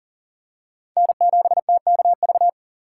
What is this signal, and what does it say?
Morse code, heard as a keyed tone, about 39 wpm, 700 Hz. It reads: N7TKV
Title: USA Call Signs - 5 Characters - 3x Character Spacing - 40wpm